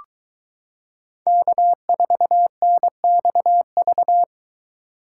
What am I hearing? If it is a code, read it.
Morse code, heard as a keyed tone, about 22 wpm, 700 Hz. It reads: K4NXV